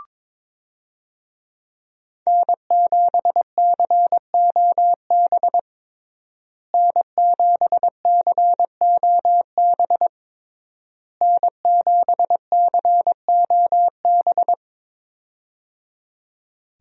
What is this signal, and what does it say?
Morse code, heard as a keyed tone, about 22 wpm, 700 Hz. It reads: N7COB N7COB N7COB
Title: USA Call Signs - 5 Characters - Repeated 3x - 22wpm